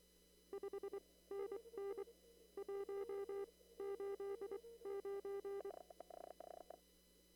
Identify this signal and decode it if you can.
Morse code, heard as a keyed tone, about 24 wpm, spaced as for 18 wpm, 400 Hz. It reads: 5NN 189
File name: CW / kt5j wpx cw-op-k5pi